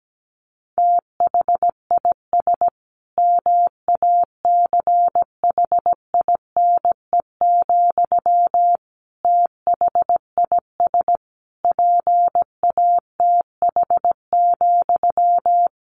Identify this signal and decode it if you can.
Morse code, heard as a keyed tone, about 17 wpm, 700 Hz. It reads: THIS MACHINE, THIS PATH,
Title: A Sound Of Thunder 17wpm